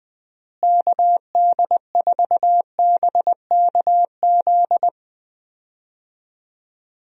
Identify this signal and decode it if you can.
Morse code, heard as a keyed tone, about 20 wpm, 700 Hz. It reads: KD4BKZ
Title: USA Call Signs - 6 Characters - Repeated 3x - 20wpm